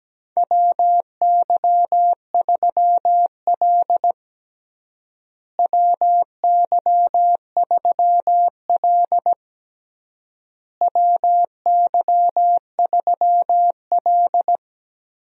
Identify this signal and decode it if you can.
Morse code, heard as a keyed tone, about 17 wpm, 700 Hz. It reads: WY3L WY3L WY3L
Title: USA Call Signs - 4 Characters - Repeated 3x - 17wpm